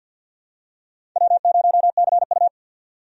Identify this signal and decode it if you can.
Morse code, heard as a keyed tone, about 52 wpm, 700 Hz. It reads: W0CU
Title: USA Call Signs - 4 Characters - 50wpm